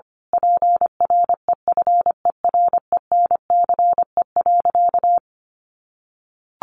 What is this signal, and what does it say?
Morse code, heard as a keyed tone, about 25 wpm, 700 Hz. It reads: PREFERENCE.
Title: Sentences from Top 1500 Words 25wpm